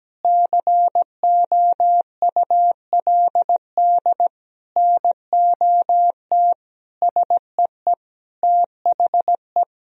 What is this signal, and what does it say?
Morse code, heard as a keyed tone, about 17 wpm, 700 Hz. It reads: COULD NOT SEE THE
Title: Sentences from Top 1000 Words - Review Part 2 - 17wpm